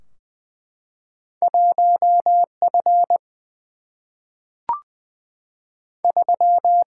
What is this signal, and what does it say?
Morse code, heard as a keyed tone, about 20 wpm, 700 Hz. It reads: 1 F 3
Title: Single-Letter Single-Number 20wpm